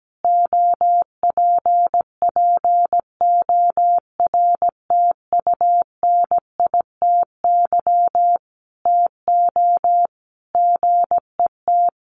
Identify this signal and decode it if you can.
Morse code, heard as a keyed tone, about 17 wpm, 700 Hz. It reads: OPPORTUNITY TO GET
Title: Sentences from Top 1400 Words 17wpm